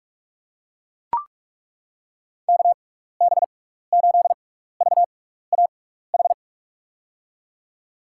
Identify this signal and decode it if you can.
Morse code, heard as a keyed tone, about 42 wpm, spaced as for 15 wpm, 700 Hz. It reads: KB8VAH
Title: USA Call Signs - 6 Characters - 6x Character Spacing - 45wpm